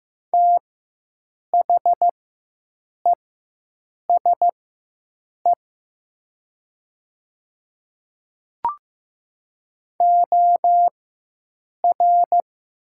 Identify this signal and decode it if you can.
Morse code, heard as a keyed tone, about 15 wpm, spaced as for 7 wpm, 700 Hz. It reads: THESE OR
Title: 4x Character Spacing - Top 200 Words - 15wpm